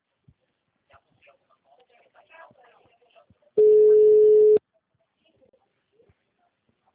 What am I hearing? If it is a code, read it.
Ringback tone (Europe, 425 Hz), with speech.